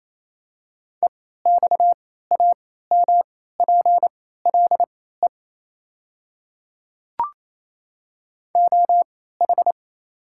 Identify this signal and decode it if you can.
Morse code, heard as a keyed tone, about 28 wpm, spaced as for 16 wpm, 700 Hz. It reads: EXAMPLE OH
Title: 3x Character Spacing - Top 500 Words - 28wpm